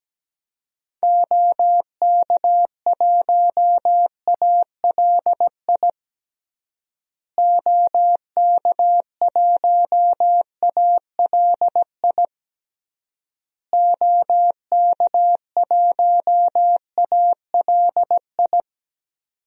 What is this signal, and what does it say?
Morse code, heard as a keyed tone, about 17 wpm, 700 Hz. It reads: OK1ALI OK1ALI OK1ALI
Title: DXCC-100 Call Signs - 6 Characters - Repeated 3x - 17wpm